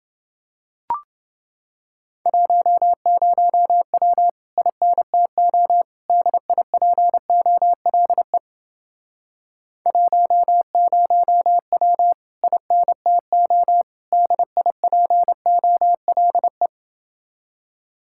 Morse code, heard as a keyed tone, about 30 wpm, 700 Hz. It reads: 10W INTO DIPOLE 10W INTO DIPOLE